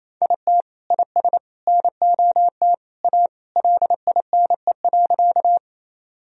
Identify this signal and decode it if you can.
Morse code, heard as a keyed tone, about 28 wpm, 700 Hz. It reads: IT IS NOT A LINE.
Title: Sentences from Top 1000 Words - Review Part 1 - 28wpm